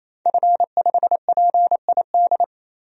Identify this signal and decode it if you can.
Morse code, heard as a keyed tone, about 27 wpm, 700 Hz. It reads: F5PID
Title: DXCC-100 Call Signs - 5 Characters - 28wpm